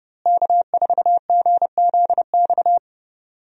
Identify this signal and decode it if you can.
Morse code, heard as a keyed tone, about 30 wpm, 700 Hz. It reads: K4GZX
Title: USA Call Signs - 5 Characters - Repeated 2x - 30wpm